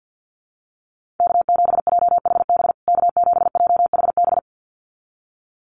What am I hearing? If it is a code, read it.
Morse code, heard as a keyed tone, about 50 wpm, 700 Hz. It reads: K7JHB K7JHB
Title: USA Call Signs Repeated 2x 50wpm